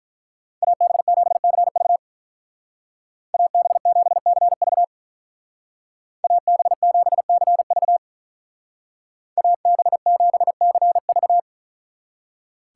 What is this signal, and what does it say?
Morse code, heard as a keyed tone, about 42 wpm, 700 Hz. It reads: AB7CV AB7CV AB7CV AB7CV
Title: Speed Racing - USA Call Signs 35wpm